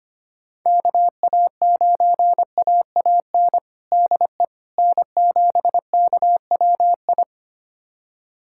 Morse code, heard as a keyed tone, about 25 wpm, 700 Hz. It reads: KA9AAN DE N7KWI
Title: USA Callsign DE Callsign 25wpm